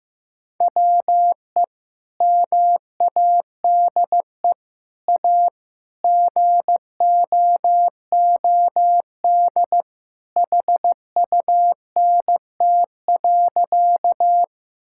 Morse code, heard as a keyed tone, about 16 wpm, 700 Hz. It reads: WE MADE A GOOD HUNT.
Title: Sentences from Top 1600 Words 15wpm